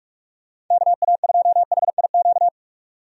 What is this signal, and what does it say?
Morse code, heard as a keyed tone, about 44 wpm, 700 Hz. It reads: KA2HIQ
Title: USA Call Signs - 6 Characters - 45wpm